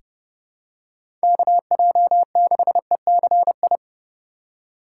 Morse code, heard as a keyed tone, about 30 wpm, 700 Hz. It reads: KJ6ECI
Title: USA Call Signs - 6 Characters - 2x Character Spacing - 30wpm